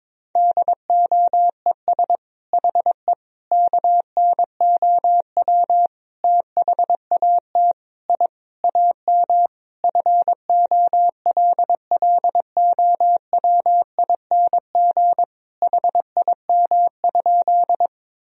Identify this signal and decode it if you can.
Morse code, heard as a keyed tone, about 22 wpm, 700 Hz. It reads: DOES HE KNOW THAT I AM FOLLOWING HIM?
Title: Sentences from Top 1000 Words - Review Part 4 - 22wpm